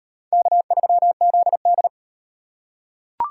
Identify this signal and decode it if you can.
Morse code, heard as a keyed tone, about 33 wpm, 700 Hz. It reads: K3ZD